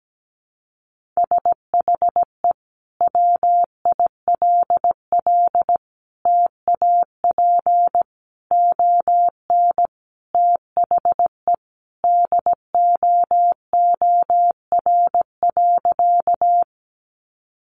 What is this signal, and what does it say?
Morse code, heard as a keyed tone, about 17 wpm, 700 Hz. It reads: SHE WILL TAP ON THE DOOR.
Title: Sentences from Top 1400 Words 17wpm